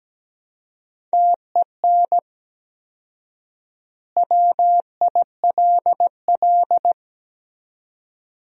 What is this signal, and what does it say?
Morse code, heard as a keyed tone, about 17 wpm, 700 Hz. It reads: TEN WILL
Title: 4x Word-Spacing - Sentences from Top 400 Words - 17wpm